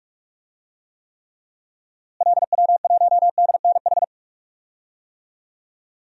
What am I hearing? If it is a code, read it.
Morse code, heard as a keyed tone, about 46 wpm, 700 Hz. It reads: LW1DNH
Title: DXCC-100 Call Signs - 6 Characters - Repeated 2x - 45wpm